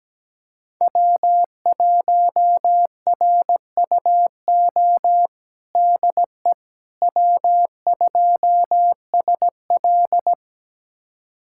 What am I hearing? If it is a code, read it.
Morse code, heard as a keyed tone, about 17 wpm, 700 Hz. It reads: W1RUO DE W2SL
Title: USA Callsign DE Callsign 17wpm